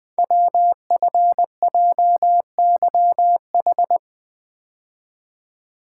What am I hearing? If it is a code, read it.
Morse code, heard as a keyed tone, about 20 wpm, 700 Hz. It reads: WFJYH
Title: Code Groups - 5 Letters - 20wpm